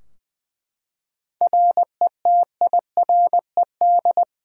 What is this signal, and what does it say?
Morse code, heard as a keyed tone, about 21 wpm, 700 Hz. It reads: RETIRED
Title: Practice QSO - Exchange 3 - More Info - 20wpm